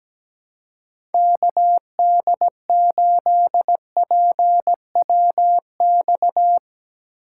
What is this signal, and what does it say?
Morse code, heard as a keyed tone, about 17 wpm, 700 Hz. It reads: KD8PWX
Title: USA Call Signs - 6 Characters - 4x Character Spacing - 17wpm